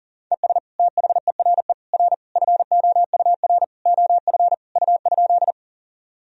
Morse code, heard as a keyed tone, about 37 wpm, 700 Hz. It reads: ES THERE R FOUR OF U?
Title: Sentences from Top 300 Words and Abbreviations 40wpm